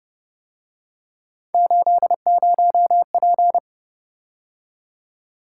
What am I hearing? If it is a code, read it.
Morse code, heard as a keyed tone, about 29 wpm, 700 Hz. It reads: 80P